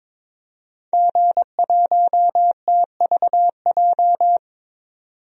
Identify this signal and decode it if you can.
Morse code, heard as a keyed tone, about 22 wpm, 700 Hz. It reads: G1TVJ
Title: DXCC-100 Call Signs - 5 Characters - 22wpm